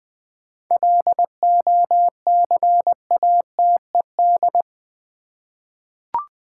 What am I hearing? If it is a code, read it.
Morse code, heard as a keyed tone, about 20 wpm, 690 Hz. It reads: LOCATED
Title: 7 Letter Words - First 3 Letters Given - 20wpm